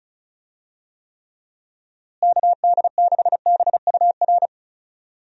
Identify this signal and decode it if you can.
Morse code, heard as a keyed tone, about 36 wpm, 700 Hz. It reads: KD6BUR
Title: USA Call Signs - 6 Characters - Repeated 2x - 35wpm